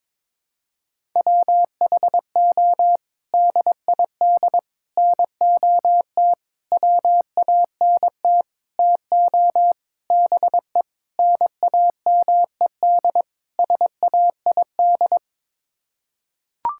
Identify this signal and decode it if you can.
Morse code, heard as a keyed tone, about 22 wpm, 700 Hz. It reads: WHO DID NOT WANT TO BE NAMED SAID
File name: practice / 22wpm / Sets of 8 Words - 22wpm